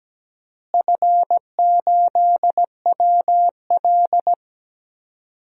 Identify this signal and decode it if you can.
Morse code, heard as a keyed tone, about 17 wpm, 700 Hz. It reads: F8WL
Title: DXCC-100 Call Signs - 4 Characters - Repeated 2x - 17wpm